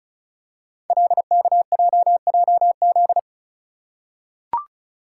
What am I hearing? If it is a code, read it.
Morse code, heard as a keyed tone, about 36 wpm, 700 Hz. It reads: LKJJZ
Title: Code Groups - 5 Letters - 35wpm